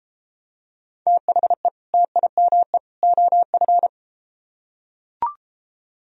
Morse code, heard as a keyed tone, about 31 wpm, 700 Hz. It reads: THE TIME OF